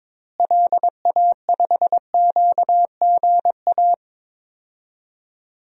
Morse code, heard as a keyed tone, about 22 wpm, 700 Hz. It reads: LA5QGA
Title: DXCC-100 Call Signs - 6 Characters - Repeated 2x - 22wpm